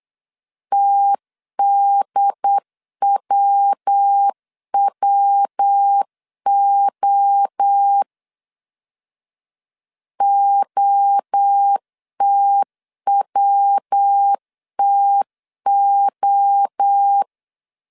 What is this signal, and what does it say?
Morse code, heard as a keyed tone, about 8 wpm, 800 Hz. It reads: TDWWO OTWTO
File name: CW / DET / 1 3 1500C S A 8wpm 4229